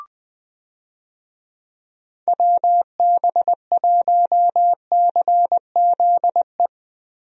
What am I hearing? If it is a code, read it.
Morse code, heard as a keyed tone, about 20 wpm, 700 Hz. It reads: WB1CZE
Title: USA Call Signs - 6 Characters - Repeated 3x - 20wpm